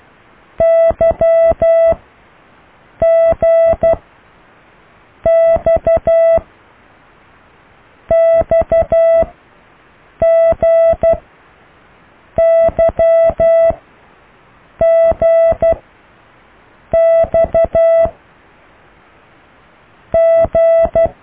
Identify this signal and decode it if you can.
Morse code, heard as a keyed tone, about 12 wpm, 660 Hz. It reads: Y G X X G Y G X G